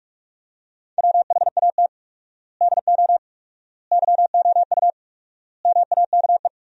Morse code, heard as a keyed tone, about 46 wpm, 700 Hz. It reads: WHAT DO YOU MAKE
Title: 4x Word-Spacing - Sentences from Top 100 Words - 45wpm